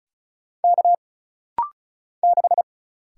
Morse code, heard as a keyed tone, about 36 wpm, 700 Hz. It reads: K 6
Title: Single Letter-Number - Rapid-Fire 35wpm